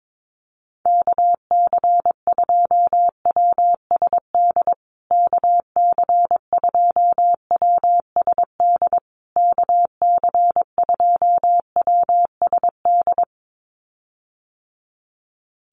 Morse code, heard as a keyed tone, about 22 wpm, 700 Hz. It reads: KC2WSD KC2WSD KC2WSD
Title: USA Call Signs Repeated 3x 22wpm